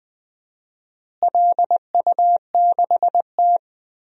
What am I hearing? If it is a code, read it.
Morse code, heard as a keyed tone, about 20 wpm, 700 Hz. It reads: LU6T
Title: DXCC-100 Call Signs - 4 Characters - Repeated 2x - 20wpm